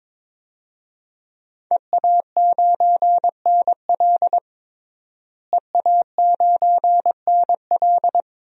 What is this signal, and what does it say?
Morse code, heard as a keyed tone, about 22 wpm, 700 Hz. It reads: EA9NL EA9NL
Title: DXCC-100 Call Signs - 5 Characters - Repeated 3x - 22wpm